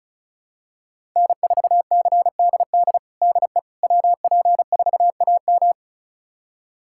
Morse code, heard as a keyed tone, about 36 wpm, 700 Hz. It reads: N4CDD DE WP4AM
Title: USA Callsign DE Callsign - 5 Characters - 35wpm